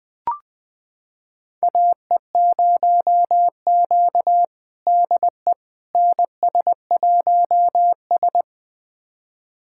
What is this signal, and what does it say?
Morse code, heard as a keyed tone, about 21 wpm, 700 Hz. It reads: AE0Q DE NS1S
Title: USA Callsign DE Callsign - 4 Characters - 20wpm